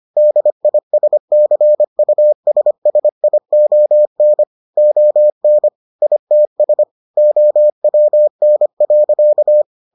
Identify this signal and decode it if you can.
Morse code, heard as a keyed tone, about 25 wpm, 600 Hz. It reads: DISCUSSION ON ITS OWN.